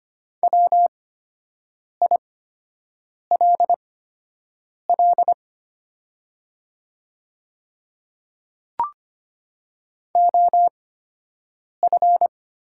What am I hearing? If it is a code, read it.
Morse code, heard as a keyed tone, about 25 wpm, spaced as for 7 wpm, 700 Hz. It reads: WILL OF
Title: 8x Character Spacing - Top 100 Words - 25wpm